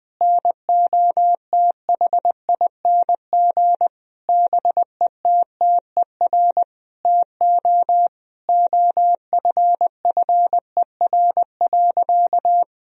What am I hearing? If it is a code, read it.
Morse code, heard as a keyed tone, about 20 wpm, 700 Hz. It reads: NOTHING BETTER TO OFFER.
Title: Sentences from Top 1000 Words - Review Part 4 - 20wpm